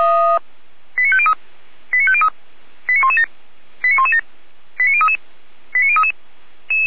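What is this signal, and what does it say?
Five-tone selective call (ZVEI), heard as quick runs of five tones, tones of about 70 ms. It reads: 89503 89503 89117 89117 89033 89033
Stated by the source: ZVEI five-tone selective call 89080 received with an SDR.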